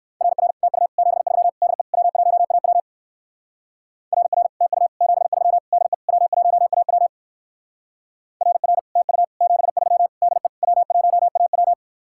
Morse code, heard as a keyed tone, about 55 wpm, 700 Hz. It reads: RR TU 73 DE W1AW RR TU 73 DE W1AW RR TU 73 DE W1AW
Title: Speed-Racing Common QSO Phrases - 45wpm